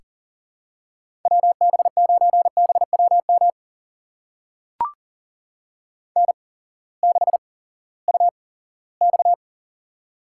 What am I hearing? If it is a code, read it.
Morse code, heard as a keyed tone, about 40 wpm, spaced as for 20 wpm, 700 Hz. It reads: WB9BWM N6UX
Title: USA Call Signs - 8x Character Spacing - 40wpm